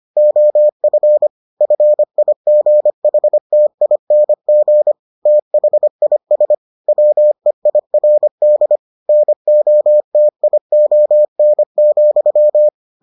Morse code, heard as a keyed tone, about 25 wpm, 600 Hz. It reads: OF FIGHTING THIS WEIRD NOTION,